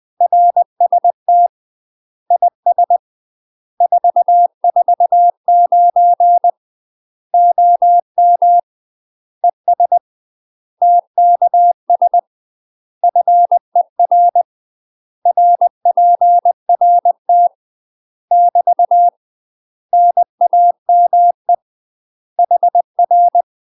Morse code, heard as a keyed tone, about 20 wpm, 700 Hz. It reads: RST IS 449 OM ES TKS FER RPRT <BT> NAME HR
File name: UK MASTER QSOs TRAINING / 20WPM / UK MASTER QSOs 0067